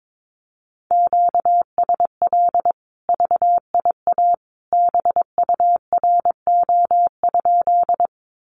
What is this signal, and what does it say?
Morse code, heard as a keyed tone, about 22 wpm, 700 Hz. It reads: QSL VIA BURO?
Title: Common QSO Phrases 22wpm